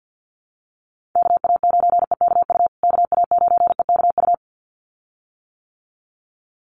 Morse code, heard as a keyed tone, about 50 wpm, 700 Hz. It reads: KA9EKU KA9EKU
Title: USA Call Signs Repeated 2x 50wpm